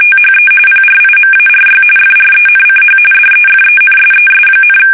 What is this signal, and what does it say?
RTTY, about 45.45 baud. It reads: THE PRIMARY FRENCH DEFENSES,
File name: fsk850 / USS-Massachusetts